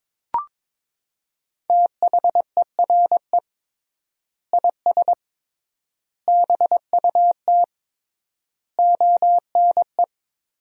Morse code, heard as a keyed tone, about 22 wpm, 700 Hz. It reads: THERE IS BUT ONE
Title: 3x Word-Spacing - Sentences from Top 100 Words - 22wpm